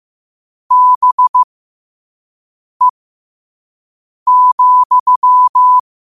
Morse code, heard as a keyed tone, about 15 wpm, spaced as for 5 wpm, 1000 Hz. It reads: BE,
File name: qotdfiles / 5wpm